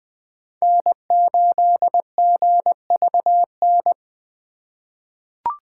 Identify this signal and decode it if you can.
Morse code, heard as a keyed tone, about 20 wpm, 700 Hz. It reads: N8GVN
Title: USA Call Signs - 5 Characters - 2x Character Spacing - 20wpm